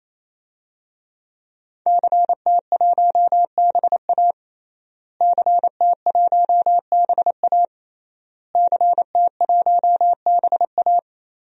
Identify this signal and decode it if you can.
Morse code, heard as a keyed tone, about 28 wpm, 700 Hz. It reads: CT1BA CT1BA CT1BA
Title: DXCC-100 Call Signs - 5 Characters - Repeated 3x - 28wpm